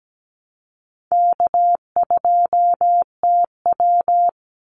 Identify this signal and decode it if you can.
Morse code, heard as a keyed tone, about 17 wpm, 700 Hz. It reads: K2TW
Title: Call Signs - 17wpm